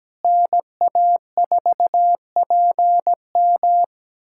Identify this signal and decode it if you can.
Morse code, heard as a keyed tone, about 17 wpm, 700 Hz. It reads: NA4PM
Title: USA Callsign DE Callsign - 5 Characters - 17wpm